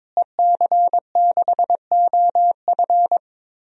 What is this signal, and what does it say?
Morse code, heard as a keyed tone, about 21 wpm, 700 Hz. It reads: EC6OF
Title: DXCC-100 Call Signs - 5 Characters - Repeated 2x - 22wpm